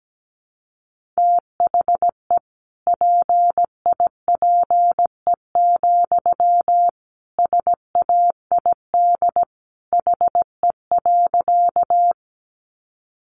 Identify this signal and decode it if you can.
Morse code, heard as a keyed tone, about 17 wpm, 700 Hz. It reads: THE PIPE, SAID HE.
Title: Sentences from Top 1500 Words 17wpm